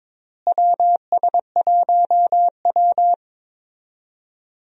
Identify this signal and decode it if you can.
Morse code, heard as a keyed tone, about 22 wpm, 700 Hz. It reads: WS1W
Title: USA Call Signs - 4 Characters - Repeated 2x - 22wpm